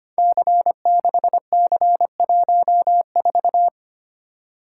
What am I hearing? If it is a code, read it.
Morse code, heard as a keyed tone, about 25 wpm, 700 Hz. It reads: C6C14
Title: Code Groups - 5 Letter:Numbers - 25wpm